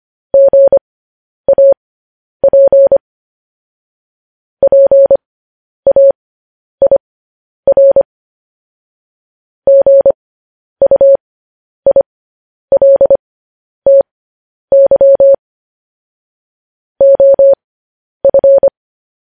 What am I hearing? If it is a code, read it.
Morse code, heard as a keyed tone, about 25 wpm, spaced as for 10 wpm, 550 Hz. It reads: GAP PAIR GUILTY OF
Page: news-headlines-in-morse-code-at-10-wpm